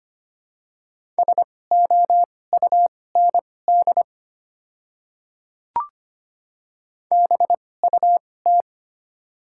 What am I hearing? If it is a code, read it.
Morse code, heard as a keyed tone, about 25 wpm, spaced as for 18 wpm, 700 Hz. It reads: SOUND BUT
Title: 2x Character Spacing - Top 100 Words - 25wpm